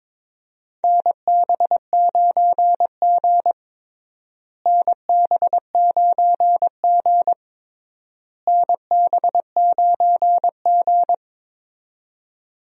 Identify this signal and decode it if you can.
Morse code, heard as a keyed tone, about 22 wpm, 700 Hz. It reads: NB9G NB9G NB9G
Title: USA Call Signs - 4 Characters - Repeated 3x - 22wpm